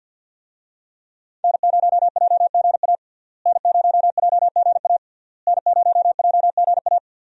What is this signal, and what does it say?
Morse code, heard as a keyed tone, about 50 wpm, 700 Hz. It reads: N0JGA N0JGA N0JGA
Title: USA Call Signs - 5 Characters - Repeated 3x - 50wpm